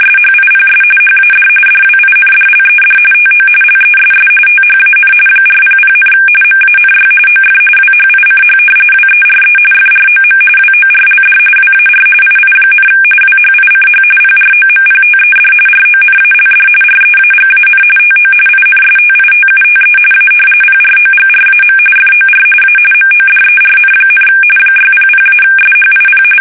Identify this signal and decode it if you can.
RTTY, about 45.45 baud. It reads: SHIPS HAD DWINDLED TO LESS THAN 400, MANY OF WHICH WERE FROM WORLD WAR II, WHICH PROMPTED RONALD REAGAN TO INSTITUTE A PROGRAM FOR A MODERN, 600-SHIP
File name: fsk850 / usnavy1